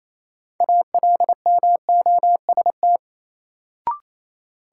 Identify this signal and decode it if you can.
Morse code, heard as a keyed tone, about 28 wpm, 700 Hz. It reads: ALMOST
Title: Sentences from Top 700 Words - New Word Drill - 28wpm